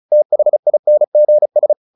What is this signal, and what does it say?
Morse code, heard as a keyed tone, about 34 wpm, 600 Hz. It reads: THINGS
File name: SS309 35